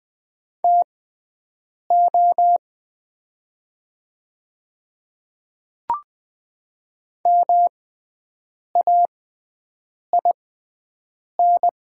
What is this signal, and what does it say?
Morse code, heard as a keyed tone, about 20 wpm, spaced as for 7 wpm, 700 Hz. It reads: TO MAIN
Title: 6x Character Spacing - Top 300 Words - 20wpm